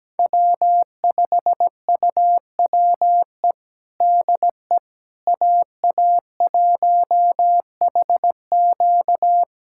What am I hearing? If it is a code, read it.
Morse code, heard as a keyed tone, about 17 wpm, 700 Hz. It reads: W5UWE DE AA1HQ
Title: USA Callsign DE Callsign - 5 Characters - 17wpm